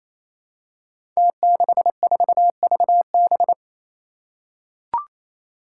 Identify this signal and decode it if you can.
Morse code, heard as a keyed tone, about 27 wpm, 700 Hz. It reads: T64VB
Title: Five Challenging Letter-Number Combinations 28wpm